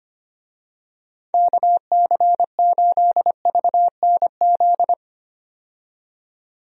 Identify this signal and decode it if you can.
Morse code, heard as a keyed tone, about 25 wpm, 700 Hz. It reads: KC8VNZ
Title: USA Call Signs - 6 Characters - 25wpm